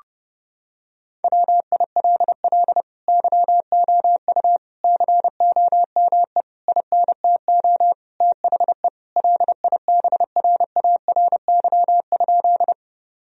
Morse code, heard as a keyed tone, about 30 wpm, 700 Hz. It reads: WILL YOU COME INTO THE LIBRARY?
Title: Sentences from Top 1000 Words - Review Part 4 - 30wpm